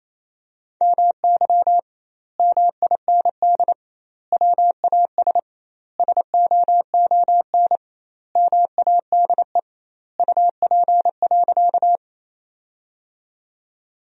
Morse code, heard as a keyed tone, about 28 wpm, 700 Hz. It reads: MY MIND WAS SOON MADE UP.
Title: 2x Word-Spacing - Sentences from Top 500 Words - 28wpm